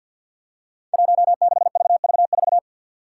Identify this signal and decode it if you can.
Morse code, heard as a keyed tone, about 52 wpm, 700 Hz. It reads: 16VV4
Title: Five Challenging Letter-Number Combinations 50wpm